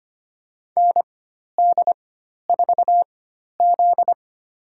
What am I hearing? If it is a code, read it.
Morse code, heard as a keyed tone, about 25 wpm, spaced as for 12 wpm, 700 Hz. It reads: ND4Z
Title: USA Call Signs - 4 Characters - 4x Character Spacing - 25wpm